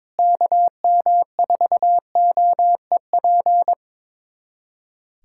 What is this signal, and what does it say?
Morse code, heard as a keyed tone, about 22 wpm, 700 Hz. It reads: KM4OEP
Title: USA Callsign DE Callsign - 6 Characters - 22wpm